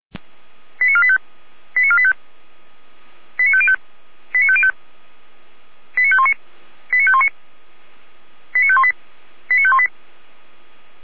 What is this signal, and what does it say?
Five-tone selective call (ZVEI), heard as quick runs of five tones, tones of about 70 ms. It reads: 89485 89485 89505 89505 89519 89519 89518 89518
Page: /455/SDR/ZVEI/89175/